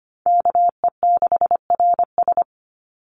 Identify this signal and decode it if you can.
Morse code, heard as a keyed tone, about 25 wpm, 700 Hz. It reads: KE6RS
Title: NA SOTA Chaser Callsigns 25wpm